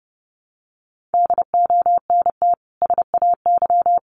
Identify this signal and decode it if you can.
Morse code, heard as a keyed tone, about 30 wpm, 700 Hz. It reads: DONT SAY
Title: Sentences from Top 300 Words 30wpm